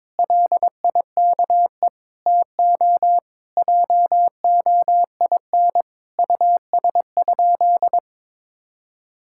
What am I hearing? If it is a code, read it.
Morse code, heard as a keyed tone, about 22 wpm, 700 Hz. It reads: LIKE TO JOIN US?
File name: practice / 22wpm / Sentences with 4-Word Phrases - 22wpm